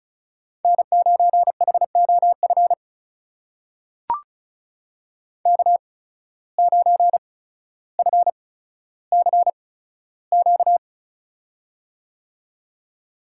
Morse code, heard as a keyed tone, about 36 wpm, spaced as for 15 wpm, 700 Hz. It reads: N9HOF K9FCQ
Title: USA Call Signs - 5 Characters - 8x Character Spacing - 35wpm